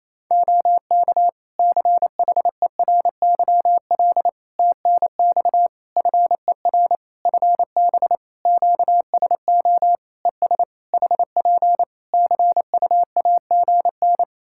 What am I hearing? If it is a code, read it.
Morse code, heard as a keyed tone, about 28 wpm, 700 Hz. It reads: OK CHERYL TNX FER FB QSO ES HP CUAGN
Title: Practice QSO - Exchange 4 - Ending - 28wpm